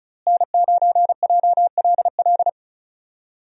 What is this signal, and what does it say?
Morse code, heard as a keyed tone, about 34 wpm, 700 Hz. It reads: N9JLL
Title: USA Call Signs - 5 Characters - Repeated 3x - 35wpm